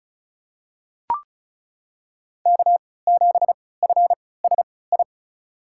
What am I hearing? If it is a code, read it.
Morse code, heard as a keyed tone, about 34 wpm, spaced as for 20 wpm, 700 Hz. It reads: K7FSI
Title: USA Call Signs - 5 Characters - 3x Character Spacing - 35wpm